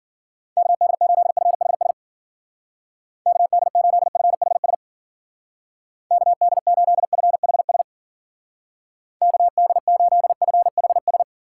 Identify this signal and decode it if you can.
Morse code, heard as a keyed tone, about 50 wpm, 700 Hz. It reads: KD8FHS KD8FHS KD8FHS KD8FHS
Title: Speed Racing - USA Call Signs 40wpm